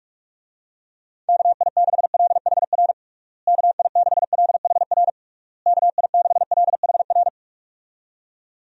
Morse code, heard as a keyed tone, about 41 wpm, 700 Hz. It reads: KI6LHR KI6LHR KI6LHR
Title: USA Call Signs - 6 Characters - Repeated 3x - 45wpm